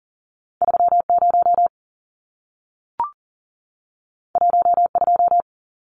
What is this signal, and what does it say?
Morse code, heard as a keyed tone, about 40 wpm, 700 Hz. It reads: 30 12
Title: Numbers - 2-Digit 40wpm